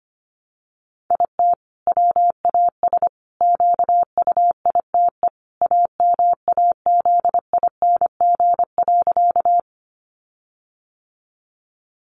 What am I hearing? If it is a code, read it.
Morse code, heard as a keyed tone, about 25 wpm, 700 Hz. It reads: IT WAS QUITE AMAZING.
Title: Sentences from Top 1500 Words 25wpm